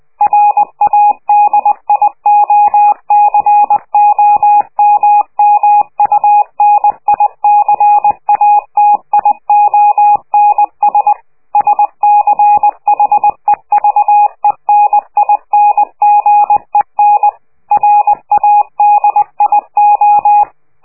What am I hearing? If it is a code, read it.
Morse code, heard as a keyed tone, about 20 wpm, 830 Hz. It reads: RADIOCOMMUNICATIONS SCHEVENINGEN RADIO
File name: worldcoastal / scheveningenradio / morsepch